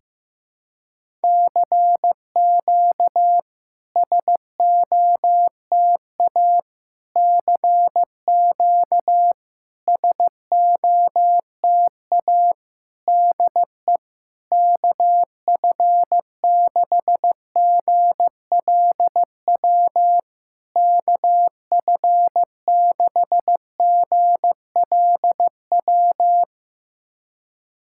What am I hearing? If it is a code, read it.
Morse code, heard as a keyed tone, about 15 wpm, 700 Hz. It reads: CQ SOTA CQ SOTA DE KF6GLW KF6GLW
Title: SOTA QSO Practice 15wpm